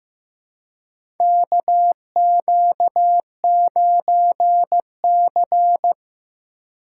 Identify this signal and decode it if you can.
Morse code, heard as a keyed tone, about 15 wpm, 700 Hz. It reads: KQ9C
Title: USA Call Signs - 4 Characters - 2x Character Spacing - 15wpm